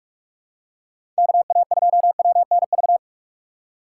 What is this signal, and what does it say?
Morse code, heard as a keyed tone, about 46 wpm, 700 Hz. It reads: KA2WNV
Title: USA Call Signs - 6 Characters - 2x Character Spacing - 45wpm